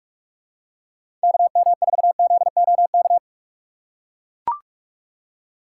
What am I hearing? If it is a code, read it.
Morse code, heard as a keyed tone, about 43 wpm, 700 Hz. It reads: KM4ZOK
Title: USA Call Signs - 6 Characters - 3x Character Spacing - 45wpm